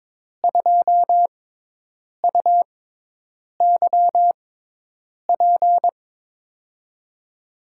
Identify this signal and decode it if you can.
Morse code, heard as a keyed tone, about 21 wpm, spaced as for 8 wpm, 700 Hz. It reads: 2UYP
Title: USA Call Signs - 5 Characters - 6x Character Spacing - 22wpm